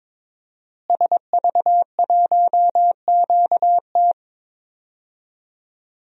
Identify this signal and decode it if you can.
Morse code, heard as a keyed tone, about 22 wpm, 700 Hz. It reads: SV1QT
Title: DXCC-100 Call Signs - 5 Characters - Repeated 3x - 22wpm